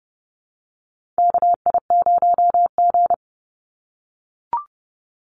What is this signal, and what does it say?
Morse code, heard as a keyed tone, about 29 wpm, 700 Hz. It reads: KI0G
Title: NA SOTA Chaser Callsigns 30wpm